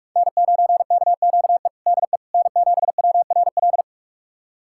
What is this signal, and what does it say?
Morse code, heard as a keyed tone, about 46 wpm, 700 Hz. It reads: N9KQE DE N7WRL
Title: USA Callsign DE Callsign - 5 Characters - 45wpm